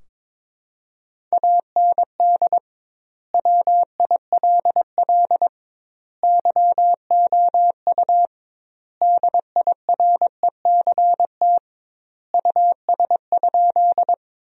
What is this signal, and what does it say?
Morse code, heard as a keyed tone, about 22 wpm, 700 Hz. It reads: AND WILL YOU DIRECT US?
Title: 2x Word-Spacing - Sentences from Top 400 Words - 22wpm